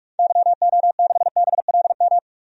Morse code, heard as a keyed tone, about 44 wpm, 700 Hz. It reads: YO6BLM
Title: DXCC-100 Call Signs - 6 Characters - 45wpm